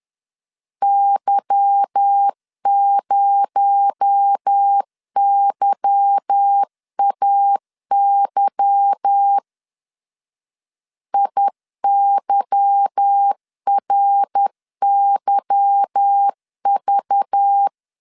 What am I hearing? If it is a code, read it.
Morse code, heard as a keyed tone, about 10 wpm, 800 Hz. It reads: Y0YAY IYRYV
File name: LEZIONI / 8 YJ45 / 8 Y 10wpm 2615